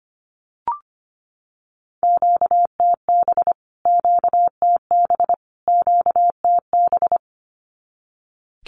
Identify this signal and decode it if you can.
Morse code, heard as a keyed tone, about 25 wpm, 700 Hz. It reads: QTB QTB QTB
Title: Q-Codes Repeated 3x 25wpm